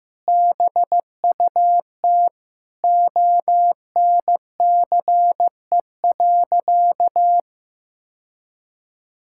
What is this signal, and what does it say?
Morse code, heard as a keyed tone, about 16 wpm, 700 Hz. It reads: BUT ONCE.
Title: Sentences from Top 1600 Words 15wpm